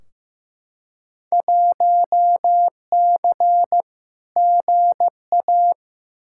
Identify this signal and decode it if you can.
Morse code, heard as a keyed tone, about 15 wpm, 700 Hz. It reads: 1C GA
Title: Field Day Exchanges Encore - 15wpm